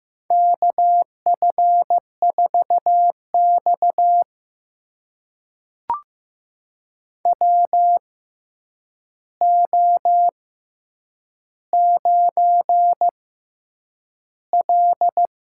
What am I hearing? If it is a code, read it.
Morse code, heard as a keyed tone, about 15 wpm, spaced as for 8 wpm, 700 Hz. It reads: KF4X WO9L
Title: USA Call Signs - 4 Characters - 6x Character Spacing - 15wpm